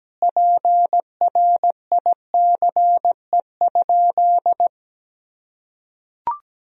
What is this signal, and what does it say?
Morse code, heard as a keyed tone, about 17 wpm, 700 Hz. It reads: PRICE?
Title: Sentences from Top 1000 Words - Review Part 3 - 17wpm